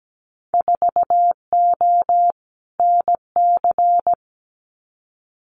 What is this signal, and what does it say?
Morse code, heard as a keyed tone, about 17 wpm, 700 Hz. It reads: 4O NC
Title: Winter Field Day Exchanges 17wpm